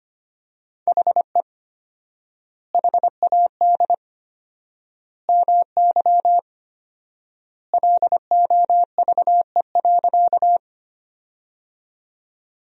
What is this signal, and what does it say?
Morse code, heard as a keyed tone, about 25 wpm, 700 Hz. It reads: HE HAD MY LOVE.
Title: 4x Word-Spacing - Sentences from Top 400 Words - 25wpm